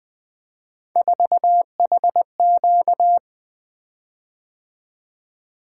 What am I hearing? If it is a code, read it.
Morse code, heard as a keyed tone, about 20 wpm, 700 Hz. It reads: 4HQ
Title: Three Letter-Number Combinations - 20wpm